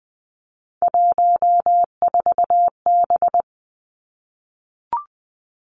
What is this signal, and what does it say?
Morse code, heard as a keyed tone, about 20 wpm, 700 Hz. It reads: 14B
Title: Three Challenging Letter-Number Combinations 20wpm